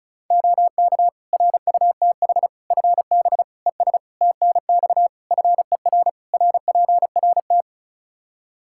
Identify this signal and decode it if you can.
Morse code, heard as a keyed tone, about 36 wpm, 700 Hz. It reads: OK RUTH FB ES TNX FER RPRT
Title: Practice QSO - Exchange 2 - Info - 35wpm